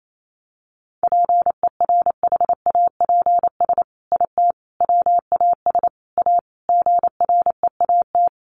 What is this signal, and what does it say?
Morse code, heard as a keyed tone, about 28 wpm, 700 Hz. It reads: PERHAPS IT WAS A GREAT
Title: Sentences from Top 1400 Words 28wpm